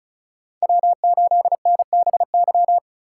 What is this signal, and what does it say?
Morse code, heard as a keyed tone, about 41 wpm, 700 Hz. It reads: W8NBY
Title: Speed Racing - USA 1x3 Call Signs 35wpm